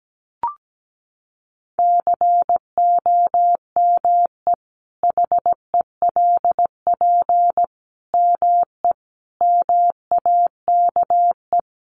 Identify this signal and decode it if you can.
Morse code, heard as a keyed tone, about 17 wpm, 700 Hz. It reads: COME HELP ME MAKE
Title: Sentences from Top 1300 Words 17wpm